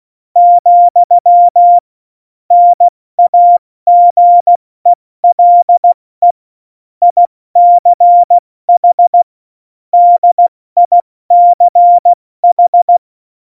Morse code, heard as a keyed tone, about 16 wpm, 700 Hz. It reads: , NAGELE ICH DICH
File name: Morsekurs / Witze / Witz 11. September 2018, 70 F80